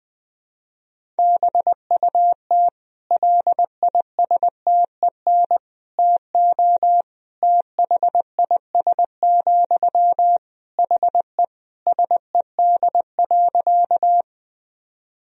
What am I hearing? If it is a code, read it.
Morse code, heard as a keyed tone, about 20 wpm, 700 Hz. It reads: BUT LISTEN TO THIS, HE SED.
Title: Sentences from Top 400 Words and Abbreviations 20wpm